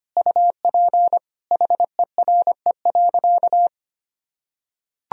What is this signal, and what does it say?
Morse code, heard as a keyed tone, about 25 wpm, 700 Hz. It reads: UP HERE.
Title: Sentences from Top 1000 Words - Review Part 1 - 25wpm